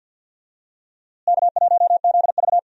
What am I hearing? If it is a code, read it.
Morse code, heard as a keyed tone, about 52 wpm, 700 Hz. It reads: K1ZV
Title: USA Call Signs - 4 Characters - Repeated 2x - 50wpm